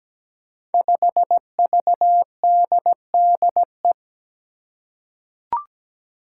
Morse code, heard as a keyed tone, about 17 wpm, 700 Hz. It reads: 5VDDE
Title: Five Challenging Letter-Number Combinations 17wpm